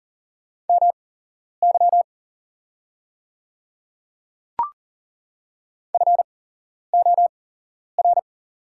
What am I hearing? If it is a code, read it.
Morse code, heard as a keyed tone, about 40 wpm, spaced as for 11 wpm, 700 Hz. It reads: MY FOR
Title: 8x Character Spacing - Top 100 Words - 40wpm